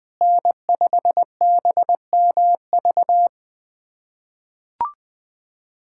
Morse code, heard as a keyed tone, about 20 wpm, 700 Hz. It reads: N5BMV
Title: USA Callsign DE Callsign - 5 Characters - 20wpm